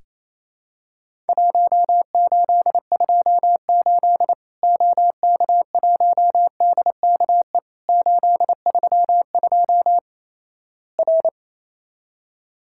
Morse code, heard as a keyed tone, about 28 wpm, 700 Hz. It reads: 1828 OK1DKE 832 R
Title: Worked All Europe DX Contest - QTC Exchange - 28wpm